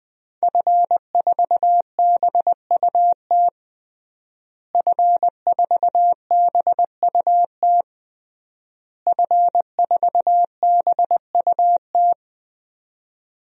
Morse code, heard as a keyed tone, about 20 wpm, 700 Hz. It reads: F4BUT F4BUT F4BUT
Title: DXCC-100 Call Signs - 5 Characters - Repeated 3x - 20wpm